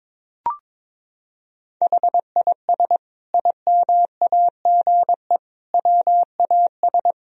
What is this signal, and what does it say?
Morse code, heard as a keyed tone, about 22 wpm, 700 Hz. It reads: HIS IMAGE WAS
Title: Sentences from Top 1000 Words - Review Part 4 - 22wpm